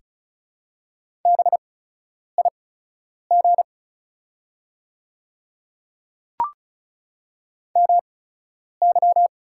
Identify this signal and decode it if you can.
Morse code, heard as a keyed tone, about 34 wpm, spaced as for 10 wpm, 700 Hz. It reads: BIG MY
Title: 8x Character Spacing - Top 300 Words - 35wpm